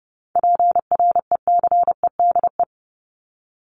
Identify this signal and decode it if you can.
Morse code, heard as a keyed tone, about 30 wpm, 700 Hz. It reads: PRECEDE
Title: ING Suffix - Top 100 Words with Base Words 30wpm